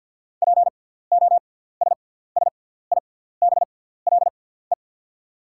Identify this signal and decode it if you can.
Morse code, heard as a keyed tone, about 48 wpm, spaced as for 17 wpm, 700 Hz. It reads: POSSIBLE
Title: 6x Character Spacing - Top 500 Words - 50wpm